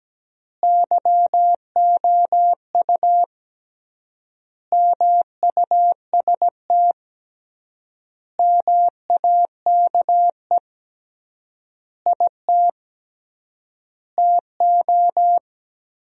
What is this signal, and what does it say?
Morse code, heard as a keyed tone, about 17 wpm, 700 Hz. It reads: YOU MUST MAKE IT TO
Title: 3x Word-Spacing - Sentences from Top 500 Words - 17wpm